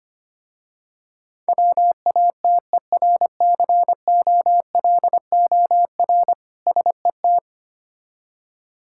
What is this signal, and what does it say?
Morse code, heard as a keyed tone, about 25 wpm, 700 Hz. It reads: WATERCOLOR SET
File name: practice / 25wpm / Difficult Words with Hints - 25wpm